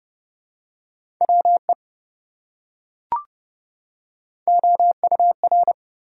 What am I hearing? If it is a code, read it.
Morse code, heard as a keyed tone, about 28 wpm, 700 Hz. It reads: WE OUR